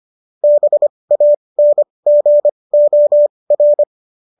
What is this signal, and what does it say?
Morse code, heard as a keyed tone, about 25 wpm, spaced as for 20 wpm, 590 Hz. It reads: BANGOR